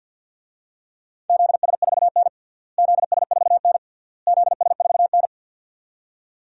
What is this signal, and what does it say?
Morse code, heard as a keyed tone, about 48 wpm, 700 Hz. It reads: ZS4N ZS4N ZS4N
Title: DXCC-100 Call Signs - 4 Characters - Repeated 3x - 50wpm